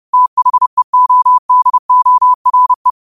Morse code, heard as a keyed tone, about 30 wpm, 1000 Hz. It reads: THEODORE